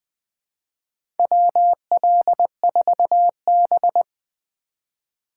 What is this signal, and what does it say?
Morse code, heard as a keyed tone, about 21 wpm, 700 Hz. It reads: WL4B
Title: USA Call Signs - 4 Characters - 2x Character Spacing - 20wpm